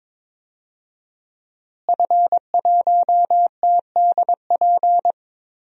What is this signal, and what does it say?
Morse code, heard as a keyed tone, about 22 wpm, 700 Hz. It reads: F1TDP
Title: DXCC-100 Call Signs - 5 Characters - Repeated 2x - 22wpm